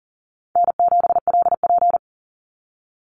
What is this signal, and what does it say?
Morse code, heard as a keyed tone, about 40 wpm, 700 Hz. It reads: N7LP
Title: NA SOTA Activator Callsigns 40wpm